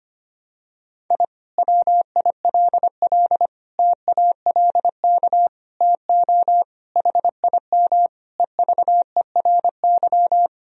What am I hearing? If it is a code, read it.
Morse code, heard as a keyed tone, about 25 wpm, 700 Hz. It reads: I WILL TALK TO HIM EVERY
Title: Sentences from Top 1000 Words - Review Part 2 - 25wpm